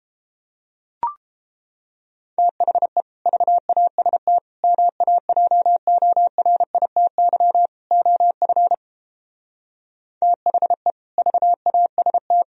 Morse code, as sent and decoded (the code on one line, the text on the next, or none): - .... / ...- .- ... - / -- .- .--- --- .-. .. - -.-- / --- ..-. / - .... / ...- .- ... -
THE VAST MAJORITY OF THE VAST